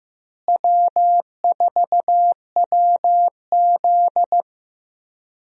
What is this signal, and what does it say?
Morse code, heard as a keyed tone, about 15 wpm, 700 Hz. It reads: W4WZ
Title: USA Call Signs - 4 Characters - Repeated 3x - 15wpm